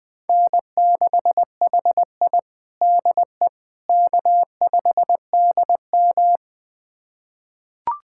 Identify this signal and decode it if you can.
Morse code, heard as a keyed tone, about 20 wpm, 700 Hz. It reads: N6HI DE K5DM
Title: USA Callsign DE Callsign - 4 Characters - 20wpm